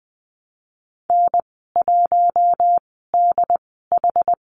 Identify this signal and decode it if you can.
Morse code, heard as a keyed tone, about 20 wpm, 700 Hz. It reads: N 1 D H
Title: 082 Lesson 14g - Number 2 - Call Signs 20wpm